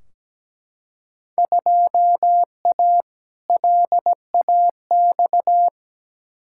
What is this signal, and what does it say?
Morse code, heard as a keyed tone, about 17 wpm, 700 Hz. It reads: 2A LAX
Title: Field Day Exchanges - 17wpm